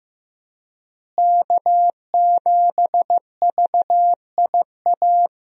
Morse code, heard as a keyed tone, about 15 wpm, 700 Hz. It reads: K7VIA
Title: USA Call Signs - 5 Characters - Repeated 2x - 15wpm